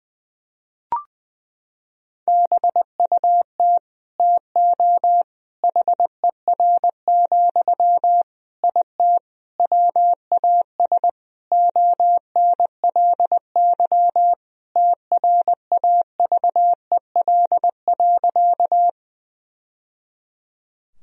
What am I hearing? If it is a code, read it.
Morse code, heard as a keyed tone, about 20 wpm, 700 Hz. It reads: BUT TO HER, IT WAS ONLY TRAVEL.
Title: Sentences from Top 1000 Words - Review Part 2 - 20wpm